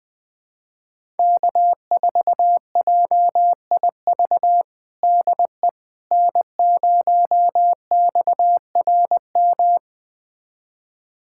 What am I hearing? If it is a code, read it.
Morse code, heard as a keyed tone, about 21 wpm, 700 Hz. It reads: K4JIV DE N0XRM
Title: USA Callsign DE Callsign - 5 Characters - 20wpm